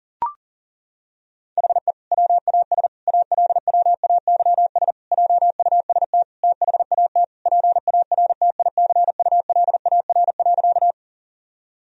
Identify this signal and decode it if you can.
Morse code, heard as a keyed tone, about 37 wpm, 700 Hz. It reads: HE WAS ALWAYS JUST THAT PARTICULAR.
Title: Sentences from Top 1000 Words - Review Part 4 - 40wpm